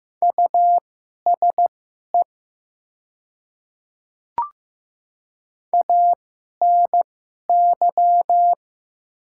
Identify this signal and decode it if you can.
Morse code, heard as a keyed tone, about 15 wpm, spaced as for 11 wpm, 700 Hz. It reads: USE ANY
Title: 2x Character Spacing - Top 200 Words - 15wpm